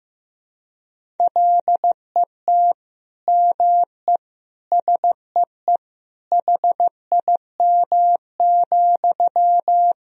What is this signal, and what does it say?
Morse code, heard as a keyed tone, about 14 wpm, 700 Hz. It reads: LET ME SEE HIM,
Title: Sentences from Top 400 Words and Abbreviations 15wpm